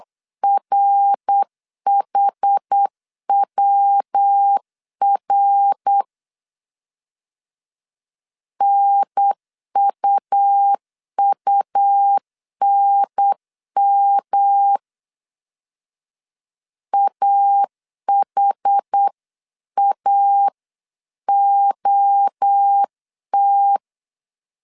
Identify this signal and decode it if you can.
Morse code, heard as a keyed tone, about 8 wpm, 800 Hz. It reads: RHWR NUUNM AHAOT